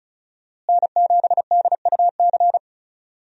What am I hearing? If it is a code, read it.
Morse code, heard as a keyed tone, about 34 wpm, 700 Hz. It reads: N7DUC
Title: USA Call Signs - 5 Characters - Repeated 3x - 35wpm